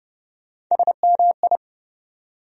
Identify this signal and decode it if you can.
Morse code, heard as a keyed tone, about 30 wpm, 700 Hz. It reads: SMI
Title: USA Call Sign Postfixes - 3 Letters 30wpm